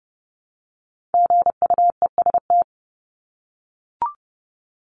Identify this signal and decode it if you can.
Morse code, heard as a keyed tone, about 30 wpm, 700 Hz. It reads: GUEST
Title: Top 1501-1750 Words 30wpm